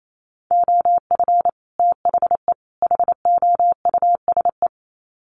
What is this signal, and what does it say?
Morse code, heard as a keyed tone, about 28 wpm, 700 Hz. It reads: OF THE HOUSE
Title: Sets of 3 Words 28wpm